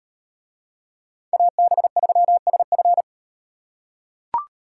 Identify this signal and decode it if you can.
Morse code, heard as a keyed tone, about 38 wpm, 700 Hz. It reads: AB3SF